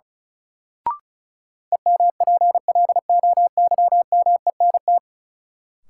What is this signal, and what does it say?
Morse code, heard as a keyed tone, about 34 wpm, 700 Hz. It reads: EMPLOYMENT
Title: Sentences from Top 1500 Words - New Word Drill - 35wpm